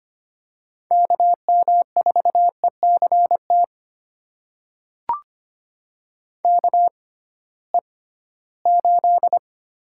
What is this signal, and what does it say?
Morse code, heard as a keyed tone, about 25 wpm, spaced as for 16 wpm, 700 Hz. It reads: KM4ECT KE8
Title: USA Call Signs - 6x Character Spacing - 25wpm